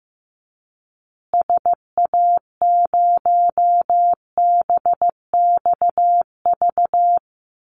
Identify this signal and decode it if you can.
Morse code, heard as a keyed tone, about 15 wpm, 700 Hz. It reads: SA0BXV
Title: Call Signs Repeated 3x - 15wpm